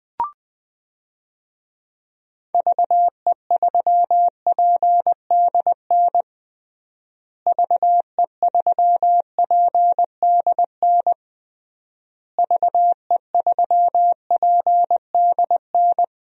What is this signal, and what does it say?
Morse code, heard as a keyed tone, about 20 wpm, 700 Hz. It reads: VE3PDN VE3PDN VE3PDN
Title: DXCC-100 Call Signs - 6 Characters - Repeated 3x - 20wpm